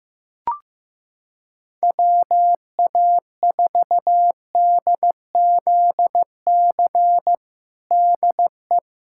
Morse code, heard as a keyed tone, about 15 wpm, 700 Hz. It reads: WA4DZC DE
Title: USA Callsign DE Callsign - 6 Characters - 15wpm